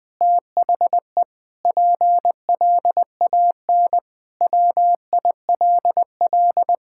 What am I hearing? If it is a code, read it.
Morse code, heard as a keyed tone, about 20 wpm, 700 Hz. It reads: THE PLAN WILL
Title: Sentences with 8-Word Phrases - Spoken Beginning - 20wpm